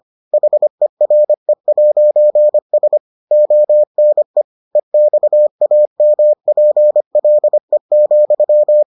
Morse code, heard as a keyed tone, about 25 wpm, 600 Hz. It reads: HERE'S ONE EXAMPLE,